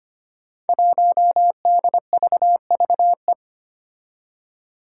Morse code, heard as a keyed tone, about 25 wpm, 700 Hz. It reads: 1DVVE
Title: Five Challenging Letter-Number Combinations 25wpm